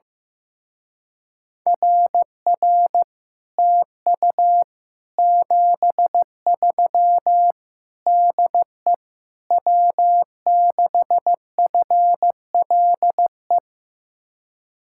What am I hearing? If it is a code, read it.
Morse code, heard as a keyed tone, about 15 wpm, 700 Hz. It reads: RR TU 73 DE W6FLE
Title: Speed-Racing Common QSO Phrases Encore - 15wpm